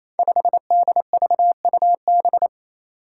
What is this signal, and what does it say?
Morse code, heard as a keyed tone, about 28 wpm, 700 Hz. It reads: 5DVUB
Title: Five Challenging Letter-Number Combinations 28wpm